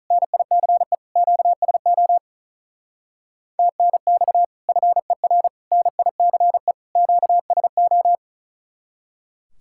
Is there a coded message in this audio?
Yes: Morse code from 0.1 to 8.2 s, heard as a keyed tone, about 47 wpm, 700 Hz. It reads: NICE QSO TNX FER NICE QSO